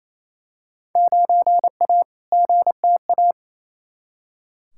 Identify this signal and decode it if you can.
Morse code, heard as a keyed tone, about 28 wpm, 700 Hz. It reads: 9A GTA
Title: Field Day Exchanges Encore - 28wpm